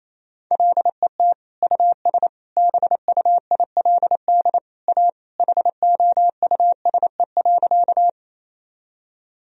Morse code, heard as a keyed tone, about 28 wpm, 700 Hz. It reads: LET US BUILD A HOUSE.
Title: Sentences from Top 1000 Words - Review Part 1 - 28wpm